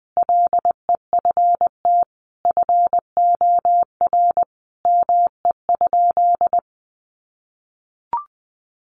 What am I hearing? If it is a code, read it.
Morse code, heard as a keyed tone, about 20 wpm, 700 Hz. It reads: LEFT FOR ME?
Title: Sentences from Top 1500 Words 20wpm